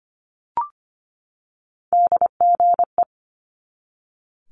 Morse code, heard as a keyed tone, about 25 wpm, 700 Hz. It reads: DGE
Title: Multi-letter Phonemes 25wpm